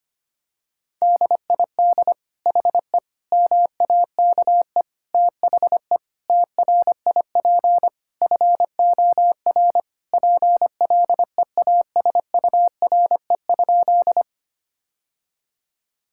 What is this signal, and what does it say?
Morse code, heard as a keyed tone, about 25 wpm, 700 Hz. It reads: DID HE MAKE THE TRIP FOR PLEASURE?
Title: Sentences from Top 1600 Words 25wpm